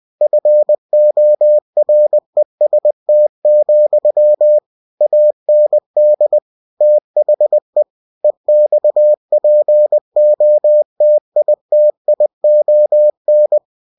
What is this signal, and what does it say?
Morse code, heard as a keyed tone, about 20 wpm, 600 Hz. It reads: FOREST, AND THE EXPOTITION
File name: cw-books / winnie-the-pooh / winnie-20wpm-page-0079